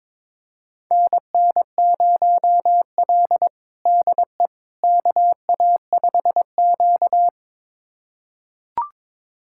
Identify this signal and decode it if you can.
Morse code, heard as a keyed tone, about 22 wpm, 700 Hz. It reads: NN0L DE KA5Q
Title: USA Callsign DE Callsign - 4 Characters - 22wpm